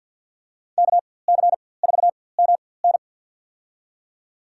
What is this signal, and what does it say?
Morse code, heard as a keyed tone, about 52 wpm, spaced as for 23 wpm, 700 Hz. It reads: KC4MN
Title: USA Call Signs - 5 Characters - 4x Character Spacing - 50wpm